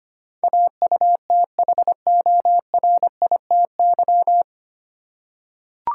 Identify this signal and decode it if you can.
Morse code, heard as a keyed tone, about 25 wpm, 700 Hz. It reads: AUTHORITY
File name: practice / 25wpm / 9 Letter Words - First 3 Letters Given - 25wpm